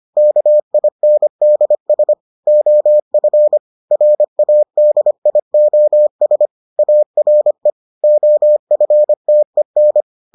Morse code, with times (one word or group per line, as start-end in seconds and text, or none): 0.2-2.1 KINDS
2.5-3.6 OF
3.9-6.5 RADIOS
6.8-7.7 ARE
8.0-10.0 OFTEN